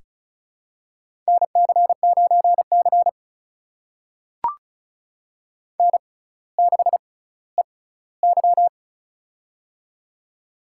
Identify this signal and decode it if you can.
Morse code, heard as a keyed tone, about 36 wpm, spaced as for 18 wpm, 700 Hz. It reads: NC9C N6EY
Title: USA Call Signs - 6x Character Spacing - 35wpm